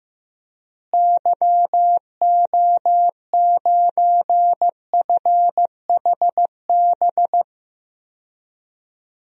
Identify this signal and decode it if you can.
Morse code, heard as a keyed tone, about 15 wpm, 700 Hz. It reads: YO9FHB
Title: DXCC-100 Call Signs - 6 Characters - Repeated 2x - 15wpm